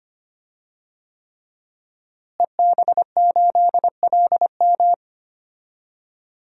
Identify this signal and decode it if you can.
Morse code, heard as a keyed tone, about 25 wpm, 700 Hz. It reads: EB8LM
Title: DXCC-100 Call Signs - 5 Characters - Repeated 3x - 25wpm